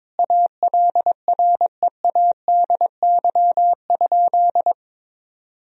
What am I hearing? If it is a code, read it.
Morse code, heard as a keyed tone, about 22 wpm, 700 Hz. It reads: ALREADY?
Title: Sentences from Top 1000 Words - Review Part 3 - 22wpm